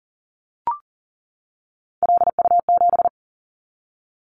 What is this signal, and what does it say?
Morse code, heard as a keyed tone, about 42 wpm, 700 Hz. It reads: LU7
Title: Callsign Prefixes for Top 50 DXCC - 40wpm